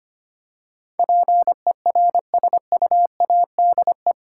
Morse code, heard as a keyed tone, about 25 wpm, 690 Hz. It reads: PERSUADE
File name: 8 Letter Words - First 3 Letters Given - 25wpm